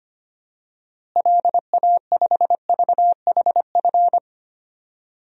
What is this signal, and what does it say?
Morse code, heard as a keyed tone, about 25 wpm, 700 Hz. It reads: LA5VHF
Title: DXCC-100 Call Signs - 6 Characters - Repeated 3x - 25wpm